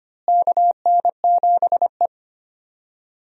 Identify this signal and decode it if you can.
Morse code, heard as a keyed tone, about 25 wpm, 700 Hz. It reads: KN7E
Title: USA Call Signs - 4 Characters - Repeated 3x - 25wpm